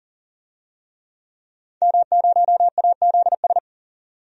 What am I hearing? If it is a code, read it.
Morse code, heard as a keyed tone, about 40 wpm, 700 Hz. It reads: M0AZS
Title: DXCC-100 Call Signs - 5 Characters - 40wpm